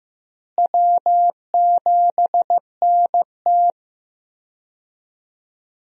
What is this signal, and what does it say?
Morse code, heard as a keyed tone, about 15 wpm, 700 Hz. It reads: W7NT
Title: USA Call Signs - 4 Characters - Repeated 3x - 15wpm